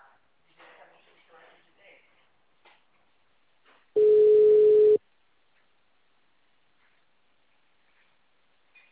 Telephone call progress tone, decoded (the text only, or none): ringback tone